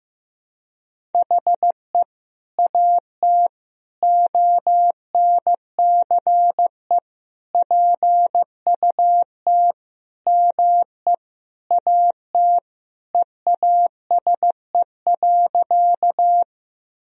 Morse code, heard as a keyed tone, about 15 wpm, 700 Hz. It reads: HE AT ONCE PUT ME AT EASE.
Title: Sentences from Top 1000 Words - Review Part 1 - 15wpm